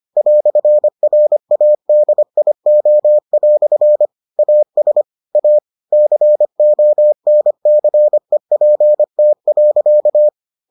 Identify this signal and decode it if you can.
Morse code, heard as a keyed tone, about 25 wpm, 600 Hz. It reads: "RADIO" AS A CONCEPT.